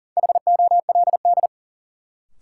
Morse code, heard as a keyed tone, about 40 wpm, 700 Hz. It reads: HOLD
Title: Sentences from Top 400 Words - New Word Drill - 40wpm